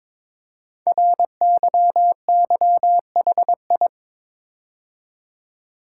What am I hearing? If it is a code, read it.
Morse code, heard as a keyed tone, about 22 wpm, 700 Hz. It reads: RYYHI
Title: Code Groups - 5 Letters - 22wpm